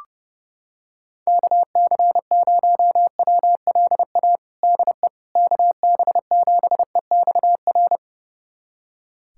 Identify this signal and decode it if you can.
Morse code, heard as a keyed tone, about 30 wpm, 700 Hz. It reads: KC0WLA DE KB7EXR
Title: USA Callsign DE Callsign 30wpm